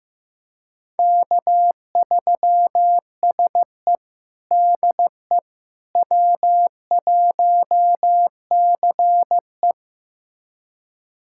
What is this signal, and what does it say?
Morse code, heard as a keyed tone, about 15 wpm, 700 Hz. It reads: K3SE DE W1CE
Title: USA Callsign DE Callsign - 4 Characters - 15wpm